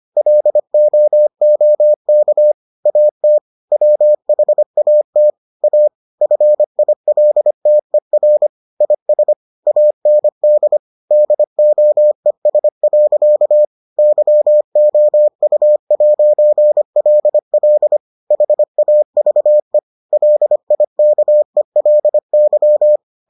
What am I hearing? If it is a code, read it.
Morse code, heard as a keyed tone, about 25 wpm, 600 Hz. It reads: LOOK AT WHAT A FILTER IS AND DOES. YOU'LL HAVE LIKELY